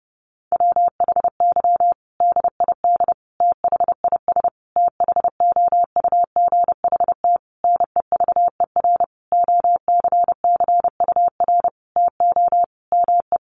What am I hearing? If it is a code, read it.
Morse code, heard as a keyed tone, about 30 wpm, 700 Hz. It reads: WHY DID THIS THOUGHT NEVER OCCUR TO ME
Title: Sentences from Top 900 Words 30wpm